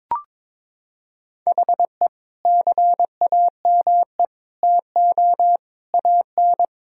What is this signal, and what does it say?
Morse code, heard as a keyed tone, about 22 wpm, 700 Hz. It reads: HE CAME TO AN
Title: Sentences from Top 1000 Words - Review Part 3 - 22wpm